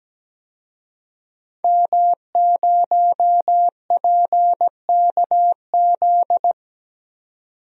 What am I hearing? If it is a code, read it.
Morse code, heard as a keyed tone, about 17 wpm, 700 Hz. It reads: M0PKZ
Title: DXCC-100 Call Signs - 5 Characters - 17wpm